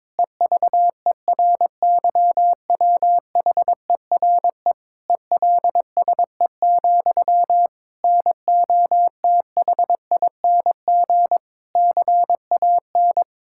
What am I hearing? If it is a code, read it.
Morse code, heard as a keyed tone, about 22 wpm, 700 Hz. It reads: EVERYWHERE ELSE, NOTHING CAN
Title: Sentences from Top 1600 Words 22wpm